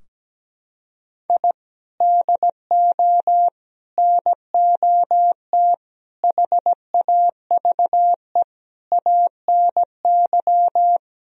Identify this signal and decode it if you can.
Morse code, heard as a keyed tone, about 17 wpm, 700 Hz. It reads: I DO NOT HAVE ANY
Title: Sentences from Top 1000 Words - Review Part 4 - 17wpm